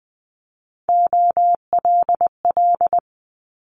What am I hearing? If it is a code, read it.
Morse code, heard as a keyed tone, about 20 wpm, 700 Hz. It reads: OLL
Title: Multi-letter Phonemes Quick Fire 20wpm